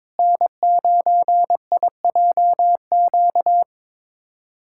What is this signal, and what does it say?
Morse code, heard as a keyed tone, about 22 wpm, 700 Hz. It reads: N9IJQ
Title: USA Callsign DE Callsign - 5 Characters - 22wpm